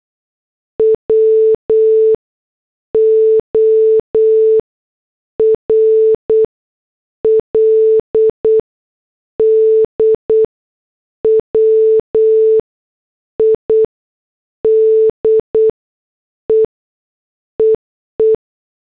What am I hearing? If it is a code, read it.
Morse code, heard as a keyed tone, about 8 wpm, spaced as for 6 wpm, 440 Hz. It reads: WORLDWIDE EE
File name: morse / headlines